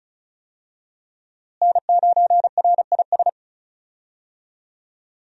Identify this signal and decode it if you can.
Morse code, heard as a keyed tone, about 34 wpm, 700 Hz. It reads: N9RIS
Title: USA Call Signs - 5 Characters - Repeated 2x - 35wpm